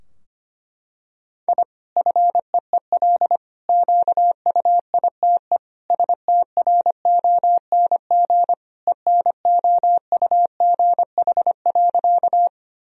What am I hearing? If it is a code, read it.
Morse code, heard as a keyed tone, about 25 wpm, 700 Hz. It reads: I FEEL QUITE STRONG ENOUGH.
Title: Sentences from Top 1000 Words - Review Part 3 - 25wpm